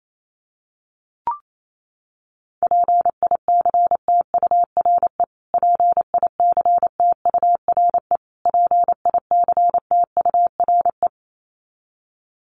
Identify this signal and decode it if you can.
Morse code, heard as a keyed tone, about 28 wpm, 700 Hz. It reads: PICTURE PICTURE PICTURE
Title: Top 500 Words Repeated 28wpm